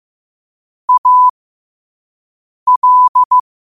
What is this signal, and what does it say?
Morse code, heard as a keyed tone, about 15 wpm, spaced as for 5 wpm, 1000 Hz. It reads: AL
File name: qotdfiles / 5wpm